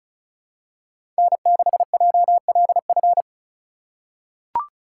Morse code, heard as a keyed tone, about 36 wpm, 700 Hz. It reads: N6JLF
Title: USA Call Signs - 5 Characters - 4x Character Spacing - 35wpm